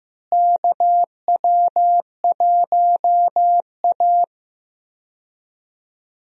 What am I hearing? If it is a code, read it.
Morse code, heard as a keyed tone, about 15 wpm, 700 Hz. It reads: KW1A
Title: USA Call Signs - 4 Characters - Repeated 2x - 15wpm